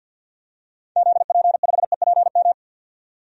Morse code, heard as a keyed tone, about 51 wpm, 700 Hz. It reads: ZP5EPM
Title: DXCC-100 Call Signs - 6 Characters - 50wpm